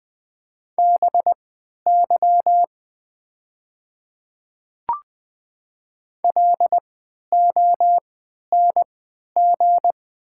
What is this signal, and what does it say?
Morse code, heard as a keyed tone, about 20 wpm, spaced as for 11 wpm, 700 Hz. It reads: BY LONG
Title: 3x Character Spacing - Top 100 Words - 20wpm